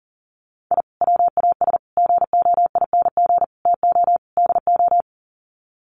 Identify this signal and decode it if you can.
Morse code, heard as a keyed tone, about 40 wpm, 700 Hz. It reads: I WAS GOING TO DO
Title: Sets of 5 Words Encore2 40wpm